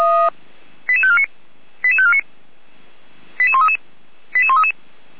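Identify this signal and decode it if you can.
Five-tone selective call (ZVEI), heard as quick runs of five tones, tones of about 70 ms. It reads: 88549 88549 88133 88133
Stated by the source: ZVEI five-tone call 88549 received by SDR, recorded by feuerwehr-orlamuende.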